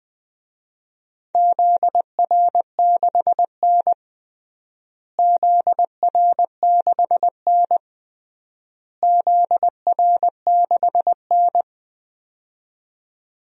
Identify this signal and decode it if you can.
Morse code, heard as a keyed tone, about 20 wpm, 700 Hz. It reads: ZR6N ZR6N ZR6N
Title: DXCC-100 Call Signs - 4 Characters - Repeated 3x - 20wpm